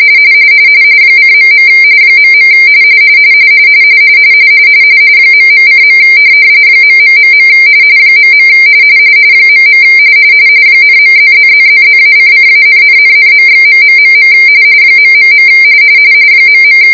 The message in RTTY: :.. .. ..:. .... .. .. ,.+++++-::-,,,,./.... ... ..